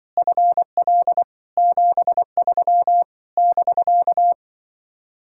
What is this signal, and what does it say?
Morse code, heard as a keyed tone, about 24 wpm, 700 Hz. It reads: FL 73 <BK>
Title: Speed-Racing Common QSO Phrases - 20wpm